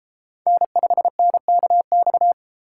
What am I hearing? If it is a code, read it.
Morse code, heard as a keyed tone, about 32 wpm, 700 Hz. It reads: N5NKX